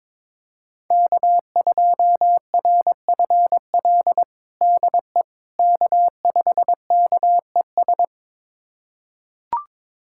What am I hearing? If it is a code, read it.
Morse code, heard as a keyed tone, about 22 wpm, 700 Hz. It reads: K2RFL DE K5KES
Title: USA Callsign DE Callsign - 5 Characters - 22wpm